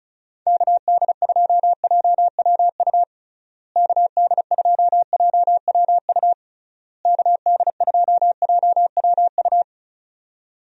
Morse code, heard as a keyed tone, about 36 wpm, 700 Hz. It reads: KD2JWU KD2JWU KD2JWU
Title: USA Call Signs - 6 Characters - Repeated 3x - 35wpm